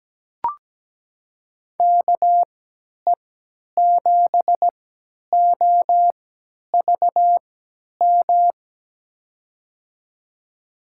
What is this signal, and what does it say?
Morse code, heard as a keyed tone, about 17 wpm, spaced as for 10 wpm, 700 Hz. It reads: KE7OVM
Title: USA Call Signs - 6 Characters - 3x Character Spacing - 17wpm